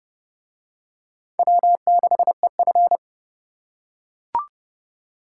Morse code, heard as a keyed tone, about 29 wpm, 700 Hz. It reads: W6EF